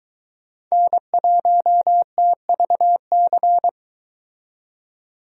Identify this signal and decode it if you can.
Morse code, heard as a keyed tone, about 22 wpm, 700 Hz. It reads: N1TVC